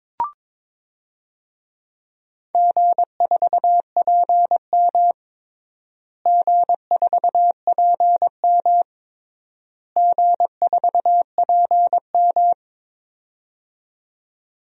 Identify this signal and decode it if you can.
Morse code, heard as a keyed tone, about 22 wpm, 700 Hz. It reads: G4PM G4PM G4PM
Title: DXCC-100 Call Signs - 4 Characters - Repeated 3x - 22wpm